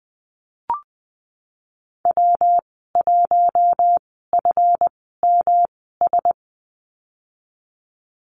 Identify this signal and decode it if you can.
Morse code, heard as a keyed tone, about 20 wpm, spaced as for 15 wpm, 700 Hz. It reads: W1FMS
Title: 186 Lesson 30g - Letter b - Call Signs 20wpm